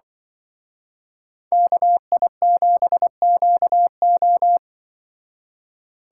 Morse code, heard as a keyed tone, about 24 wpm, 700 Hz. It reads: KI7QO
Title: Speed Racing - USA 2x2 Call Signs 20wpm